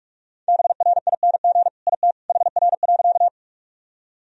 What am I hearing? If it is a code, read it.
Morse code, heard as a keyed tone, about 46 wpm, 700 Hz. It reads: BRING IT HR.
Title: Sentences from Top 500 Words and Abbreviations 45wpm